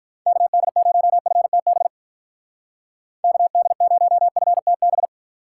Morse code, heard as a keyed tone, about 44 wpm, 700 Hz. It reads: KD0FTB KD0FTB
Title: Speed Racing - USA 2x3 Call Signs 35wpm